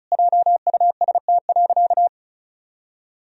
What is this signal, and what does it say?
Morse code, heard as a keyed tone, about 36 wpm, 700 Hz. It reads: JUST.
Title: Sentences from Top 1000 Words - Review Part 1 - 35wpm